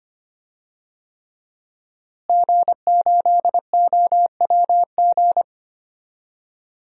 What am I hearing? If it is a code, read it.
Morse code, heard as a keyed tone, about 25 wpm, 700 Hz. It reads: G8OWG
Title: DXCC-100 Call Signs - 5 Characters - Repeated 3x - 25wpm